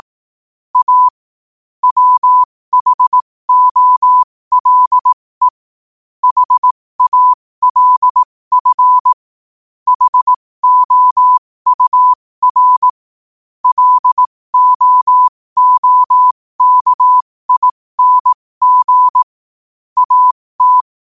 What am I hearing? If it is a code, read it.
Morse code, heard as a keyed tone, about 18 wpm, spaced as for 15 wpm, 1000 Hz. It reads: A WHOLE HALF HOUR LOOKING AT